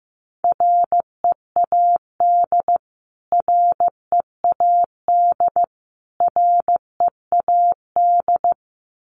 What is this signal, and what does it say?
Morse code, heard as a keyed tone, about 15 wpm, 700 Hz. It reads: READ READ READ
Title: Top 200 Words Repeated 15wpm